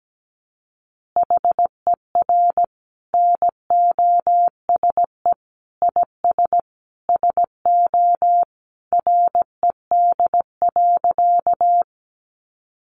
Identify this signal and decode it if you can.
Morse code, heard as a keyed tone, about 17 wpm, 700 Hz. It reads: HER NOSE IS SO RED.
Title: Easy Phrases 17wpm